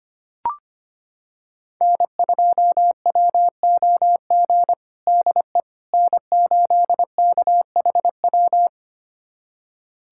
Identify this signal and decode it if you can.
Morse code, heard as a keyed tone, about 25 wpm, 700 Hz. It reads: N2WOG DE N8KHW
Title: USA Callsign DE Callsign - 5 Characters - 25wpm